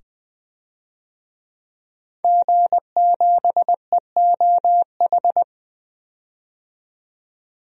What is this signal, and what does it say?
Morse code, heard as a keyed tone, about 21 wpm, 700 Hz. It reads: G7EOH
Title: DXCC-100 Call Signs - 5 Characters - 20wpm